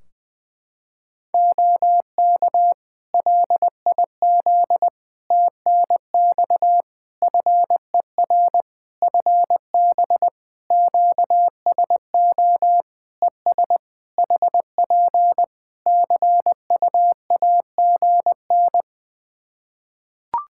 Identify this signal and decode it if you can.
Morse code, heard as a keyed tone, about 20 wpm, 700 Hz. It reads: OK LIZ TNX FER FB QSO ES HP CUAGN
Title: Practice QSO - Exchange 4 - Ending - 20wpm